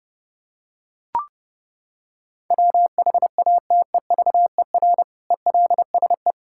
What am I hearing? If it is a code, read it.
Morse code, heard as a keyed tone, about 30 wpm, 700 Hz. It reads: WHATEVER ELSE
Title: Sentences from Top 1600 Words 30wpm